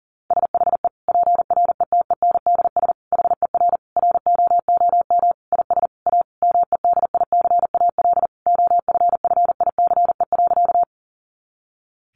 Morse code, heard as a keyed tone, about 40 wpm, 700 Hz. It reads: SHE PRETENDS HER ROOM IS A MEDICAL OFFICE.
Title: Additional Easy Phrases 40wpm